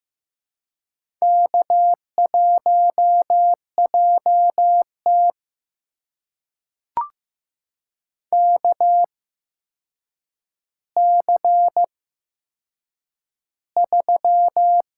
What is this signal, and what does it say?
Morse code, heard as a keyed tone, about 15 wpm, spaced as for 7 wpm, 700 Hz. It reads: K1JT KC3
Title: USA Call Signs - 4 Characters - 8x Character Spacing - 15wpm